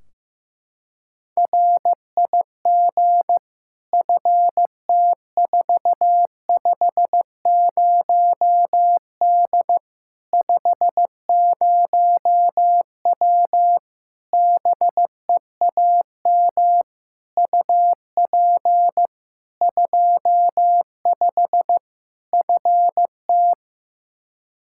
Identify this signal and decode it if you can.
Morse code, heard as a keyed tone, about 15 wpm, 700 Hz. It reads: RIG FT450D 50W BEAM UP 25 FT
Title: Practice QSO - Exchange 2 - Info - 15wpm